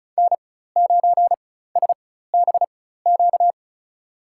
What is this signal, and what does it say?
Morse code, heard as a keyed tone, about 34 wpm, spaced as for 16 wpm, 700 Hz. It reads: N9SBQ
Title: USA Call Signs - 5 Characters - 4x Character Spacing - 35wpm